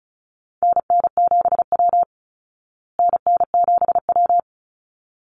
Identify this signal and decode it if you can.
Morse code, heard as a keyed tone, about 34 wpm, 700 Hz. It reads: NN7W NN7W
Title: Call Signs Repeated 3x - 35wpm